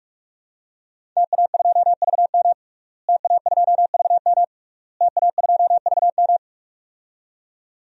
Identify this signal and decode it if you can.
Morse code, heard as a keyed tone, about 46 wpm, 700 Hz. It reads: TA2VM TA2VM TA2VM
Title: DXCC-100 Call Signs - 5 Characters - Repeated 3x - 45wpm